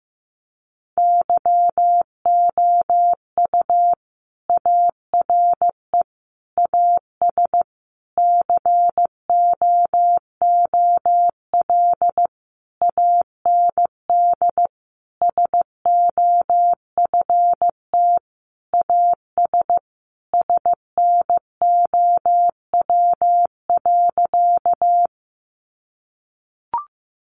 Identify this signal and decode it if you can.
Morse code, heard as a keyed tone, about 15 wpm, 700 Hz. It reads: YOU ARE AS COOL AND SOFT AS SNOW.
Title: Sentences from Top 1500 Words 15wpm